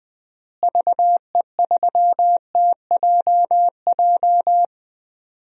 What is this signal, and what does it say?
Morse code, heard as a keyed tone, about 20 wpm, 700 Hz. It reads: VE3TJJ
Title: DXCC-100 Call Signs - 6 Characters - Repeated 3x - 20wpm